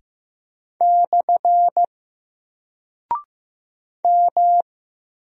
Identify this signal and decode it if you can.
Morse code, heard as a keyed tone, about 15 wpm, 700 Hz. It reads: / M
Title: Single Character (Letter:Number:Punctuation:Prosign) 15wpm